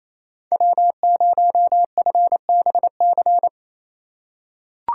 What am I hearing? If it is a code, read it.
Morse code, heard as a keyed tone, about 28 wpm, 700 Hz. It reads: W0FBC
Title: USA Call Signs - 5 Characters - 2x Character Spacing - 28wpm